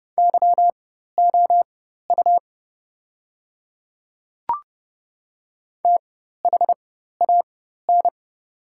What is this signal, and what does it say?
Morse code, heard as a keyed tone, about 30 wpm, spaced as for 14 wpm, 700 Hz. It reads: YOU THAN
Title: 4x Character Spacing - Top 100 Words - 30wpm